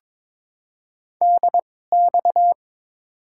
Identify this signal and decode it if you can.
Morse code, heard as a keyed tone, about 22 wpm, spaced as for 16 wpm, 700 Hz. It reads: DX
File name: Top 90 QSO Elements - 2x Character Spacing - 22wpm